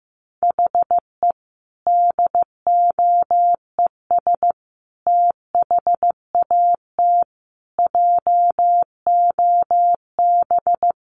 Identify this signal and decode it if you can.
Morse code, heard as a keyed tone, about 15 wpm, 700 Hz. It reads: HE DOES THAT JOB
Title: Sentences from Top 700 Words 15wpm